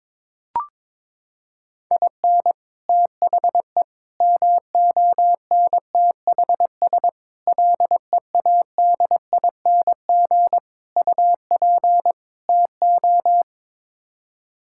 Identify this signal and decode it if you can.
Morse code, heard as a keyed tone, about 22 wpm, 700 Hz. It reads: IN THE MONTHS LEADING UP TO
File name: practice / 22wpm / Sets of 6 Words - 22wpm